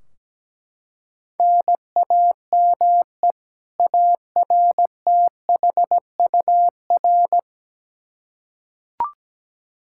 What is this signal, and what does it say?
Morse code, heard as a keyed tone, about 17 wpm, 700 Hz. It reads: NAME ARTHUR
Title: Practice QSO - Exchange 1 - Beginning - 17wpm